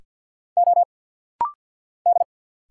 Morse code, heard as a keyed tone, about 50 wpm, 700 Hz. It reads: O D
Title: Single Letters - Rapid-Fire 50wpm